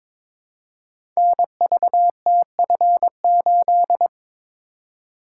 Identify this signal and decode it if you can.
Morse code, heard as a keyed tone, about 22 wpm, 700 Hz. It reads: NVTF8
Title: Code Groups - 5 Letter:Numbers - 22wpm